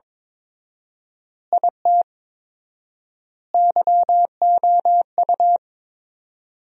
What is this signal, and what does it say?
Morse code, heard as a keyed tone, about 22 wpm, 700 Hz. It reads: IT YOU
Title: 4x Word-Spacing - Sentences from Top 200 Words - 22wpm